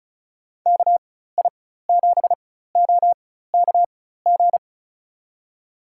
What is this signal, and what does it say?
Morse code, heard as a keyed tone, about 36 wpm, spaced as for 16 wpm, 700 Hz. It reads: KI7OKG
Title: USA Call Signs - 6 Characters - 4x Character Spacing - 35wpm